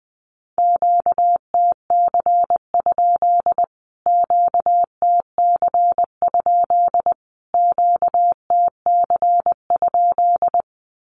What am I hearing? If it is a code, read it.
Morse code, heard as a keyed tone, about 20 wpm, 700 Hz. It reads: QTC? QTC? QTC?
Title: Q-Codes Repeated 3x 20wpm